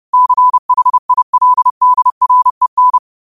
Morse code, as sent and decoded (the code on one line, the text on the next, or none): -.-. .... .. .-.. -.. .-. -.
CHILDREN